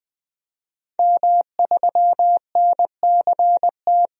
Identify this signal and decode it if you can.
Morse code, heard as a keyed tone, about 20 wpm, 700 Hz. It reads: M3NCT
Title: DXCC-100 Call Signs - 5 Characters - 20wpm